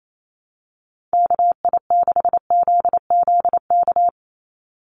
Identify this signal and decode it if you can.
Morse code, heard as a keyed tone, about 28 wpm, 700 Hz. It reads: KI6ZZK
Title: USA Call Signs 28wpm